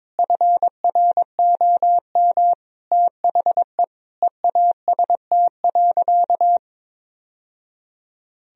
Morse code, heard as a keyed tone, about 21 wpm, 700 Hz. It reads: FROM THE EAST.
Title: Sentences from Top 1000 Words - Review Part 3 - 22wpm